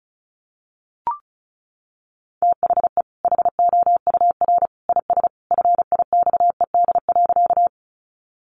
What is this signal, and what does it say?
Morse code, heard as a keyed tone, about 34 wpm, 700 Hz. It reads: THE HOUR IS FIXED.
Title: Sentences from Top 1100 Words 35wpm